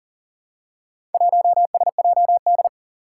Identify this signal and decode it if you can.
Morse code, heard as a keyed tone, about 40 wpm, 700 Hz. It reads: 1SJD
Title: Four Challenging Letter-Number Combinations 40wpm